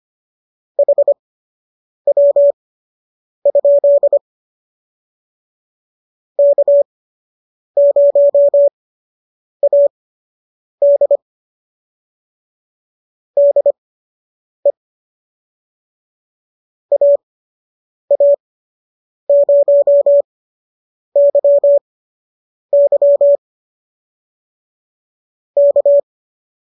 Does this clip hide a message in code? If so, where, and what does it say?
Yes: Morse code from 0.8 to 26.0 s, heard as a keyed tone, about 25 wpm, spaced as for 8 wpm, 570 Hz. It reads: HW? K0AD DE AA0YY K